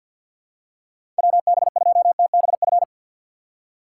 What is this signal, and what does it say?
Morse code, heard as a keyed tone, about 50 wpm, 700 Hz. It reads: WB2TBF
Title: USA Call Signs - 6 Characters - 6x Character Spacing - 50wpm